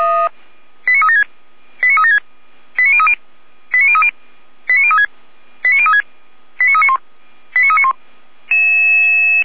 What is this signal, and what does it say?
Five-tone selective call (ZVEI), heard as quick runs of five tones, tones of about 80 ms. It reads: 89376 89376 89030 89030 89036 89036 89391 89391